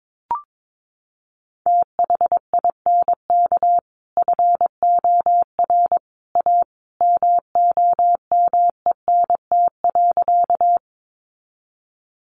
Morse code, heard as a keyed tone, about 22 wpm, 700 Hz. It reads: THINK FOR A MOMENT.
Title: Sentences from Top 1000 Words 22wpm